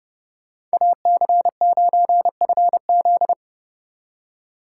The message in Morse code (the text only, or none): AC9FZ